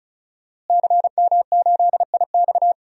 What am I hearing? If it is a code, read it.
Morse code, heard as a keyed tone, about 34 wpm, 700 Hz. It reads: CM8IX
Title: DXCC-100 Call Signs - 5 Characters - Repeated 2x - 35wpm